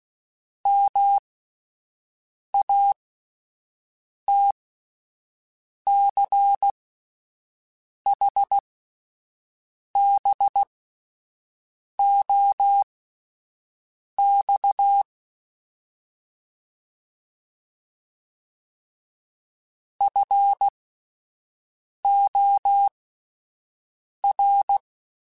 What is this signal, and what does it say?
Morse code, heard as a keyed tone, about 16 wpm, spaced as for 5 wpm, 800 Hz. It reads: MATCHBOX FOR
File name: (spaced) / QSOs Spaced-mp3s / WD0FBT 1 @10 sp